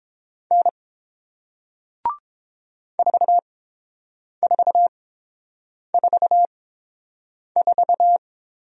Morse code, heard as a keyed tone, about 30 wpm, spaced as for 7 wpm, 700 Hz. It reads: N 4444